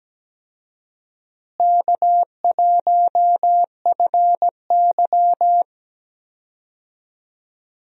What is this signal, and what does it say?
Morse code, heard as a keyed tone, about 17 wpm, 700 Hz. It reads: K1FY
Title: USA Call Signs - 4 Characters - Repeated 2x - 17wpm